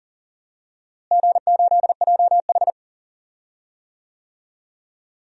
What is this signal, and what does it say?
Morse code, heard as a keyed tone, about 40 wpm, 700 Hz. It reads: G8JH
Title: DXCC-100 Call Signs - 4 Characters - Repeated 3x - 40wpm